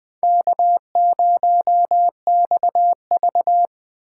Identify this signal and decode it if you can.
Morse code, heard as a keyed tone, about 24 wpm, 700 Hz. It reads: K0XV
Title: Speed Racing - USA 1x2 Call Signs 20wpm